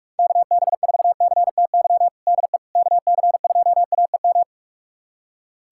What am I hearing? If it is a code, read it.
Morse code, heard as a keyed tone, about 42 wpm, 700 Hz. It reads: KB4CTY DE KC2AEM
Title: USA Callsign DE Callsign 45wpm